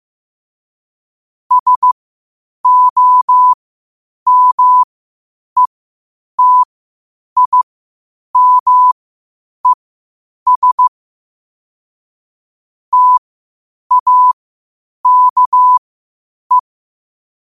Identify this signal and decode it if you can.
Morse code, heard as a keyed tone, about 15 wpm, spaced as for 8 wpm, 1000 Hz. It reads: SOMETIMES TAKE
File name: qotdfiles / 8wpm